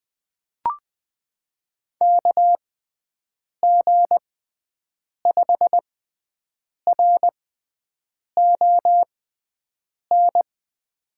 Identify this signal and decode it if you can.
Morse code, heard as a keyed tone, about 20 wpm, spaced as for 7 wpm, 700 Hz. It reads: KG5RON
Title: USA Call Signs - 6 Characters - 6x Character Spacing - 20wpm